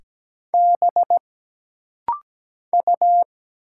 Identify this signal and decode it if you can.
Morse code, heard as a keyed tone, about 17 wpm, 700 Hz. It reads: B U
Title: Single Letters - Rapid-Fire 17wpm